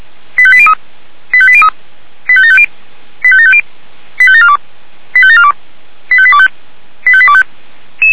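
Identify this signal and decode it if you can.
Five-tone selective call (ZVEI), heard as quick runs of five tones, tones of about 70 ms. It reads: 85803 85803 85750 85750 85742 85742 85825 85825
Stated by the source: ZVEI five-tone call received by SDR.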